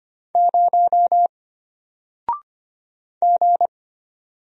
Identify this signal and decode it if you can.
Morse code, heard as a keyed tone, about 25 wpm, 700 Hz. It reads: 0 G
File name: Single-Letter-Number-25wpm